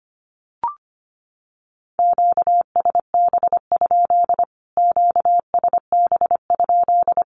Morse code, heard as a keyed tone, about 25 wpm, 700 Hz. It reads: QSB? QSB?
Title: Q-Codes Repeated 3x 25wpm